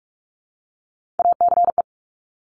Morse code, heard as a keyed tone, about 46 wpm, 700 Hz. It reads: ACE
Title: Top Three Letter Combinations in Words 45wpm